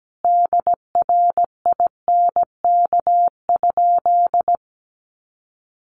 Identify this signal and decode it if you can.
Morse code, heard as a keyed tone, about 17 wpm, 700 Hz. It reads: DRINK?
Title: Sentences from Top 1500 Words 17wpm